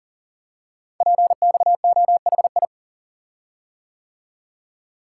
Morse code, heard as a keyed tone, about 40 wpm, 690 Hz. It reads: PXOHI
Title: Code Groups - 5 Letters - 40wpm